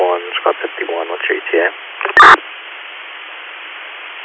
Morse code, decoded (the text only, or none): none